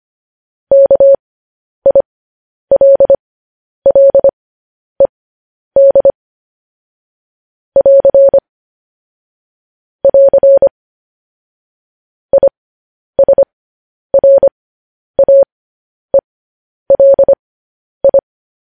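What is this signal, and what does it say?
Morse code, heard as a keyed tone, about 25 wpm, spaced as for 10 wpm, 550 Hz. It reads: KILLED <AR> <AR> ISRAELI
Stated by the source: Morse code news headlines sent at 10 wpm for practice.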